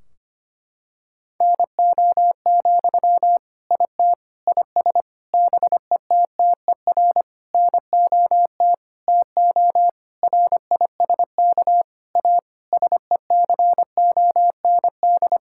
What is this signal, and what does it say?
Morse code, heard as a keyed tone, about 25 wpm, 700 Hz. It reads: NO, IT IS BETTER NOT TO RISK A SECOND
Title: Sentences from Top 1600 Words 25wpm